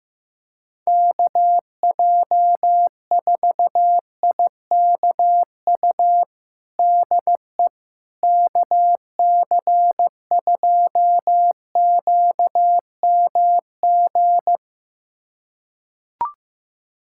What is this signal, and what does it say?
Morse code, heard as a keyed tone, about 15 wpm, 700 Hz. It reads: KJ4IKU DE KC2QMG
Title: USA Callsign DE Callsign - 6 Characters - 15wpm